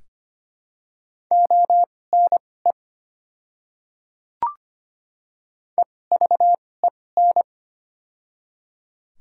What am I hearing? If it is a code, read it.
Morse code, heard as a keyed tone, about 25 wpm, spaced as for 18 wpm, 700 Hz. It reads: ONE EVEN
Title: 2x Character Spacing - Top 300 Words - 25wpm